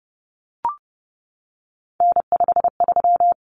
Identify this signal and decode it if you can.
Morse code, heard as a keyed tone, about 30 wpm, 700 Hz. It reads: N53
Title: Common Cut Numbers - 3-Digit 30wpm